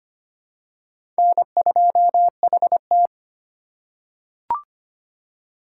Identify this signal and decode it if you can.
Morse code, heard as a keyed tone, about 25 wpm, 700 Hz. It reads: N2HT
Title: USA Call Signs - 4 Characters - 2x Character Spacing - 25wpm